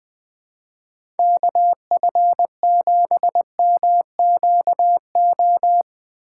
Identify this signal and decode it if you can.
Morse code, heard as a keyed tone, about 19 wpm, 700 Hz. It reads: KF7MQO
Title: USA Call Signs - 6 Characters - 4x Character Spacing - 20wpm